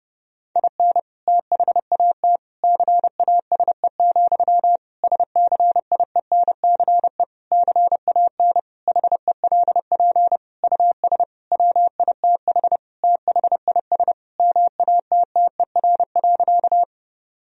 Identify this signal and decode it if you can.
Morse code, heard as a keyed tone, about 30 wpm, 700 Hz. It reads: IN THAT CASE, SCIENCE CAN HELP US WITH THIS MATTER.
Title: Sentences from Top 1000 Words - Review Part 2 - 30wpm